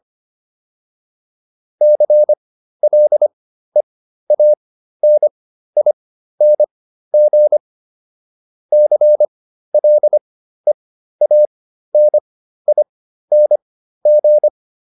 Morse code, heard as a keyed tone, about 25 wpm, spaced as for 13 wpm, 600 Hz. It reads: CLEANING CLEANING
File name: ING8 13